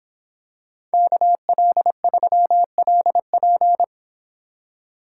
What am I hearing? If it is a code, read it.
Morse code, heard as a keyed tone, about 27 wpm, 700 Hz. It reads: KL3LP
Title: Speed Racing - USA 2x2 Call Signs 22wpm